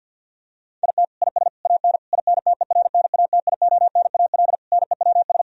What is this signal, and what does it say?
Morse code, heard as a keyed tone, about 49 wpm, 700 Hz. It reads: IT IS AN INTERNATIONAL NEWS
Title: Sentences from Top 1000 Words - Review Part 4 - 50wpm